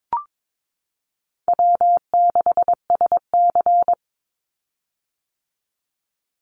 Morse code, heard as a keyed tone, about 22 wpm, 700 Hz. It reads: W6SC
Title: Easy Call Signs - 22wpm